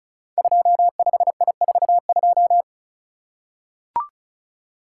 Morse code, heard as a keyed tone, about 36 wpm, 700 Hz. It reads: 25I42
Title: Code Groups - 5 Letter:Numbers - 35wpm